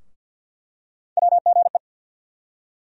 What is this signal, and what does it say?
Morse code, heard as a keyed tone, about 46 wpm, 690 Hz. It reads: WGE
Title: USA Call Sign Postfixes - 3 Letters 50wpm